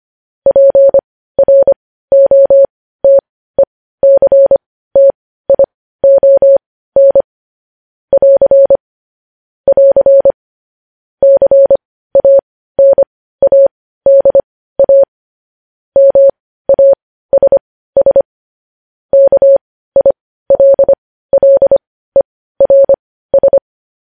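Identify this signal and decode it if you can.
Morse code, heard as a keyed tone, about 25 wpm, spaced as for 15 wpm, 550 Hz. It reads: PROTECTION <AR> <AR> CANADA MASS KILLERS
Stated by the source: Morse news headlines practice, possibly sent at 15 wpm.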